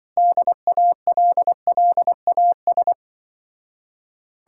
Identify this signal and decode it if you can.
Morse code, heard as a keyed tone, about 24 wpm, 700 Hz. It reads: DALLAS
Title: Speed-Racing Common QSO Phrases - 20wpm